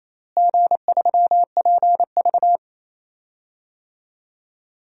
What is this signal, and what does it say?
Morse code, heard as a keyed tone, about 28 wpm, 700 Hz. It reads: G3PV
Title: DXCC-100 Call Signs - 4 Characters - Repeated 3x - 28wpm